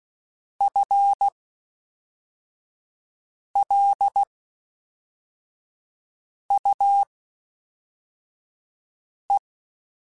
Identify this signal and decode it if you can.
Morse code, heard as a keyed tone, about 16 wpm, spaced as for 4 wpm, 800 Hz. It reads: FLUE
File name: Spaced / FL Runs sp mp3s / FL Runs sp @5